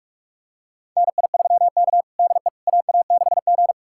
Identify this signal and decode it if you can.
Morse code, heard as a keyed tone, about 41 wpm, 700 Hz. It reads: NI3K DE AA6G
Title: USA Callsign DE Callsign - 4 Characters - 45wpm